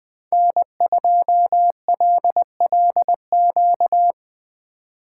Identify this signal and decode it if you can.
Morse code, heard as a keyed tone, about 22 wpm, 700 Hz. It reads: N2LLQ
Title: Speed Racing - USA 1x3 Call Signs 20wpm